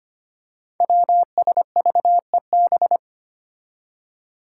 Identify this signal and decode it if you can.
Morse code, heard as a keyed tone, about 25 wpm, 700 Hz. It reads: WSVEB
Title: Five Challenging Letter-Number Combinations 25wpm